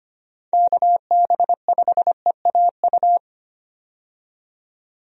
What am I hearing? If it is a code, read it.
Morse code, heard as a keyed tone, about 25 wpm, 700 Hz. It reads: KB5EAU
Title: USA Call Signs - 6 Characters - 25wpm